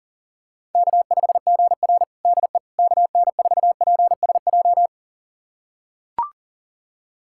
Morse code, heard as a keyed tone, about 37 wpm, 700 Hz. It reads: K5GR DE KN4PSJ
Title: USA Callsign DE Callsign 40wpm